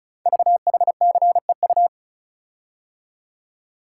Morse code, heard as a keyed tone, about 36 wpm, 700 Hz. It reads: VHCEU
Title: Code Groups - 5 Letters - 35wpm